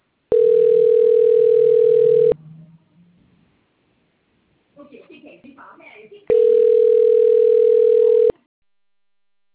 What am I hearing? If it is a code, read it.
Ringback tone (North America), 440 + 480 Hz, with speech.
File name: rg-1600-unknown- - - .48798